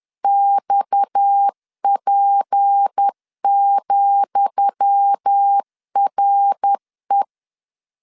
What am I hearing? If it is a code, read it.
Morse code, heard as a keyed tone, about 10 wpm, 800 Hz. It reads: XP,RE
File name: LEZIONI / 10 SIGNS / 10D COMA 10wpm 3013